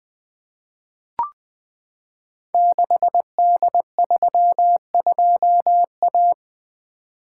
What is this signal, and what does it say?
Morse code, heard as a keyed tone, about 20 wpm, 700 Hz. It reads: 6D32A
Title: Code Groups - 5 Letter:Numbers - 20wpm